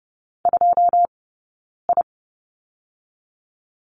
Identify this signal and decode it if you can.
Morse code, heard as a keyed tone, about 30 wpm, 700 Hz. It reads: 2 I
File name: Lesson 17d - Letter u - Two Characters 30wpm